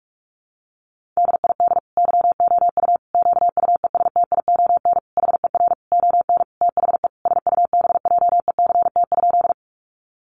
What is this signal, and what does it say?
Morse code, heard as a keyed tone, about 44 wpm, 700 Hz. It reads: DID YOU QUESTION HER ON THE SUBJECT?
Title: Sentences from Top 800 Words 45wpm